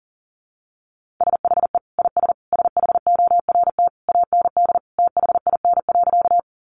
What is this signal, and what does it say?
Morse code, heard as a keyed tone, about 40 wpm, 700 Hz. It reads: SHE IS SHORT AND THIN.
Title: Sentences from Top 1200 Words 40wpm